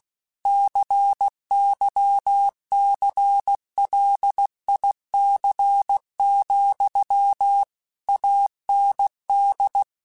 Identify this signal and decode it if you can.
Morse code, heard as a keyed tone, about 16 wpm, 800 Hz. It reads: CYCLIC, AND
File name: Technical Subjects / Technical Subjects-mp3s / XL @15